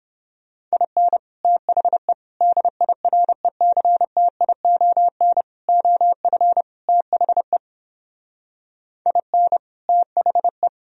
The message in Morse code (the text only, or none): IN THE DIRECTION OF THE IN THE